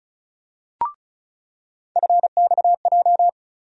Morse code, heard as a keyed tone, about 36 wpm, 700 Hz. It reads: FXJ
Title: USA Call Sign Postfixes - 3 Letters 35wpm